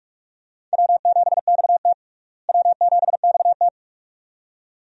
Morse code, heard as a keyed tone, about 42 wpm, 700 Hz. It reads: W7XT W7XT
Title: USA Call Signs - 4 Characters - Repeated 2x - 45wpm